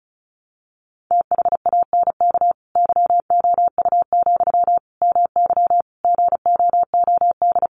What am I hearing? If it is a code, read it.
Morse code, heard as a keyed tone, about 34 wpm, 700 Hz. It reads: THANK YOU, MY GOOD
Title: Sentences from Top 600 Words 35wpm